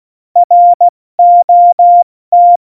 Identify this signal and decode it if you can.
Morse code, heard as a keyed tone, about 16 wpm, 700 Hz. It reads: ROT
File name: Morsekurs / Witze / Witz 19. Januar 2021, 70 F80